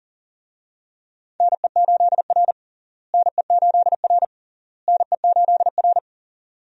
Morse code, heard as a keyed tone, about 40 wpm, 700 Hz. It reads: NE8R NE8R NE8R
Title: USA Call Signs - 4 Characters - Repeated 3x - 40wpm